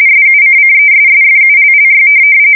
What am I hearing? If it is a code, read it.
RTTY, about 45.45 baud. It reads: SUCCESSFULLY